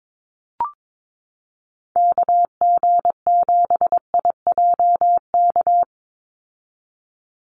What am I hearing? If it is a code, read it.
Morse code, heard as a keyed tone, about 22 wpm, 700 Hz. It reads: KG7IJK
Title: USA Call Signs 22wpm